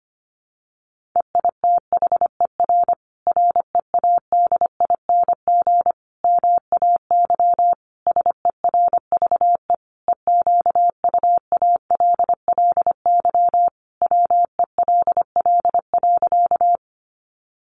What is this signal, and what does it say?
Morse code, heard as a keyed tone, about 25 wpm, 700 Hz. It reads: EITHER READING MAY SERVE EQUALLY WELL.
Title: Sentences from Top 1400 Words 25wpm